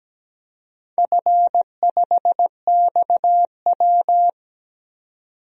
Morse code, heard as a keyed tone, about 17 wpm, 700 Hz. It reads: F5XW
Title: DXCC-100 Call Signs - 4 Characters - 17wpm